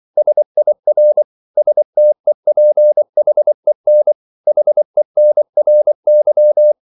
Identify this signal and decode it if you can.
Morse code, heard as a keyed tone, about 24 wpm, 600 Hz. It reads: SIR STEPHEN HENRY
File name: CW / Practice / EnglishQuotations0424